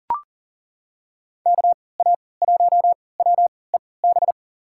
Morse code, heard as a keyed tone, about 40 wpm, spaced as for 23 wpm, 700 Hz. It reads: KA1WEB
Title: USA Call Signs - 3x Character Spacing - 40wpm